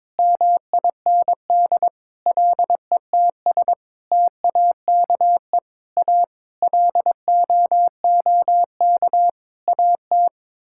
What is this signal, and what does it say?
Morse code, heard as a keyed tone, about 22 wpm, 700 Hz. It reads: MIND LETS TAKE A LOOK AT
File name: Sets of 9 Words - 22wpm